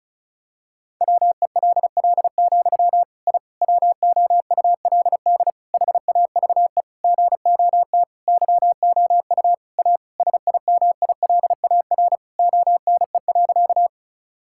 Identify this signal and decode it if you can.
Morse code, heard as a keyed tone, about 34 wpm, 700 Hz. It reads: WELL, I WOULD HAVE GOT YOU A SIMILAR ONE.
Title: Sentences from Top 1000 Words - Review Part 5 - 35wpm